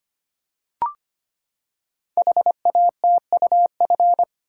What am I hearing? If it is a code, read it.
Morse code, heard as a keyed tone, about 25 wpm, 700 Hz. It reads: HATUF
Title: Code Groups - 5 Letters - 25wpm